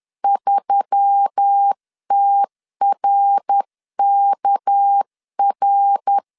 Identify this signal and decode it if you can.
Morse code, heard as a keyed tone, about 10 wpm, 800 Hz. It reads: 3TRKR
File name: CW / DET / 1 6 1500C S C 10wpm 4252